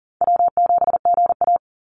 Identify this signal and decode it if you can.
Morse code, heard as a keyed tone, about 40 wpm, 700 Hz. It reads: W7GA
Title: NA SOTA Chaser Callsigns 40wpm